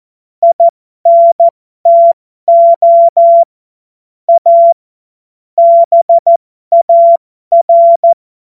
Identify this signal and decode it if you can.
Morse code, heard as a keyed tone, about 14 wpm, spaced as for 12 wpm, 680 Hz. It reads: INTO A BAR
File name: Morsekurs / Witze